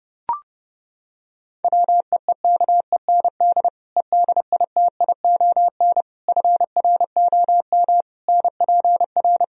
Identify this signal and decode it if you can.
Morse code, heard as a keyed tone, about 29 wpm, 700 Hz. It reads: WEEKEND EDITION FROM NPR